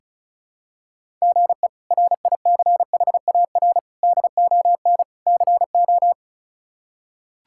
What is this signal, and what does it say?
Morse code, heard as a keyed tone, about 36 wpm, 700 Hz. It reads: GE RICHAR DON CO
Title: K1USN SST QSO Practice - 35wpm